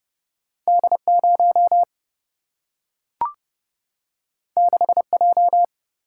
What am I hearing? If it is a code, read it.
Morse code, heard as a keyed tone, about 29 wpm, 700 Hz. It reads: D 0 6 J
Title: Single-Letter Single-Number 30wpm